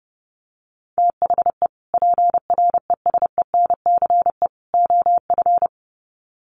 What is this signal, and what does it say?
Morse code, heard as a keyed tone, about 30 wpm, 700 Hz. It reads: THE PRESENCE OF
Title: Sets of 3 Words 30wpm